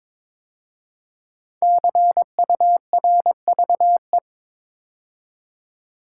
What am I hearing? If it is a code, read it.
Morse code, heard as a keyed tone, about 22 wpm, 700 Hz. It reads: CURVE
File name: 5 Letter Words - First 3 Letters Given - 22wpm